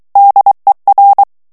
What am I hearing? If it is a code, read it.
Morse code, heard as a keyed tone, about 23 wpm, 800 Hz. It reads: DER
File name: ALLEMAND texte clair 1 vitesse 22 mots minute